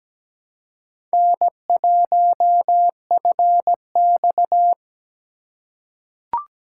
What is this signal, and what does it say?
Morse code, heard as a keyed tone, about 17 wpm, 700 Hz. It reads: N1FX
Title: USA Call Signs - 4 Characters - 4x Character Spacing - 17wpm